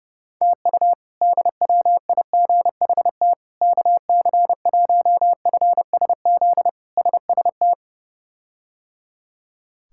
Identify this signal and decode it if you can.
Morse code, heard as a keyed tone, about 32 wpm, 700 Hz. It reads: TU DWIGHT KC1FSZ SST
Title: K1USN SST QSO Practice - 30wpm